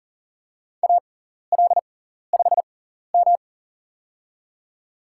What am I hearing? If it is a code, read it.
Morse code, heard as a keyed tone, about 39 wpm, spaced as for 14 wpm, 700 Hz. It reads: AL5M
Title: USA Call Signs - 4 Characters - 6x Character Spacing - 40wpm